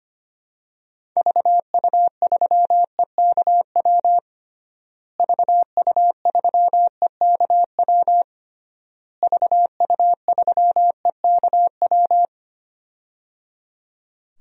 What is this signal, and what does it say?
Morse code, heard as a keyed tone, about 25 wpm, 700 Hz. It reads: VU3EKW VU3EKW VU3EKW
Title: DXCC-100 Call Signs - 6 Characters - Repeated 3x - 25wpm